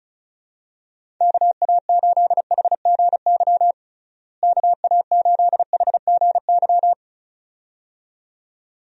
Morse code, heard as a keyed tone, about 36 wpm, 700 Hz. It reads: KA8HGY KA8HGY
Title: USA Call Signs - 6 Characters - Repeated 2x - 35wpm